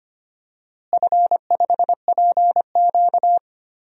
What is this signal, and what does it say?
Morse code, heard as a keyed tone, about 25 wpm, 700 Hz. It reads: F5PQ
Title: DXCC-100 Call Signs - 4 Characters - Repeated 2x - 25wpm